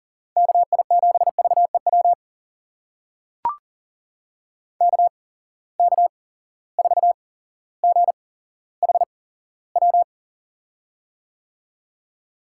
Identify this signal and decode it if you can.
Morse code, heard as a keyed tone, about 40 wpm, spaced as for 17 wpm, 700 Hz. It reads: KI7VEW KK4GHW
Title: USA Call Signs - 6 Characters - 8x Character Spacing - 40wpm